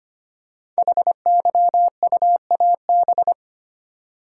Morse code, heard as a keyed tone, about 25 wpm, 700 Hz. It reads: HYUAB
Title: Code Groups - 5 Letter:Numbers - 25wpm